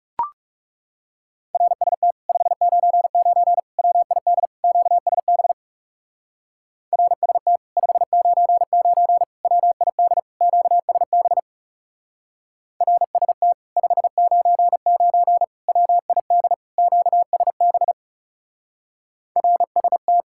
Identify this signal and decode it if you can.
Morse code, heard as a keyed tone, about 37 wpm, 700 Hz. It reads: RST 599 WID QSB RST 599 WID QSB RST 599 WID QSB RST
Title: Speed-Racing Common QSO Phrases Encore - 30wpm